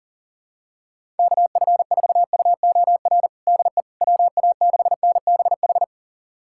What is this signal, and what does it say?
Morse code, heard as a keyed tone, about 40 wpm, 700 Hz. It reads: KF4UOR DE WA6NBH
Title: USA Callsign DE Callsign - 6 Characters - 40wpm